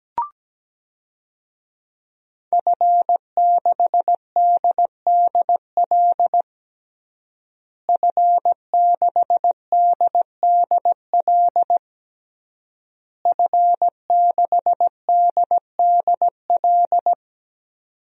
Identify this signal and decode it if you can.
Morse code, heard as a keyed tone, about 17 wpm, 700 Hz. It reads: F6DDL F6DDL F6DDL
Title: DXCC-100 Call Signs - 5 Characters - Repeated 3x - 17wpm